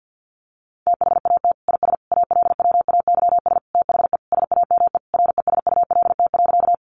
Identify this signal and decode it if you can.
Morse code, heard as a keyed tone, about 50 wpm, 700 Hz. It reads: THAT IS ALWAYS THE SAME RESULT.
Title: Sentences from Top 700 Words 50wpm